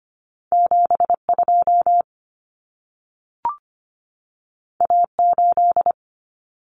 Morse code, heard as a keyed tone, about 25 wpm, 700 Hz. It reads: 72 A8
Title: Common Cut Numbers - 2-Digit 25wpm